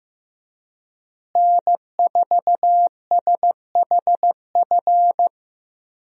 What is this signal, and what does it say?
Morse code, heard as a keyed tone, about 15 wpm, 700 Hz. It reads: N4SHF
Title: USA Call Signs - 5 Characters - Repeated 2x - 15wpm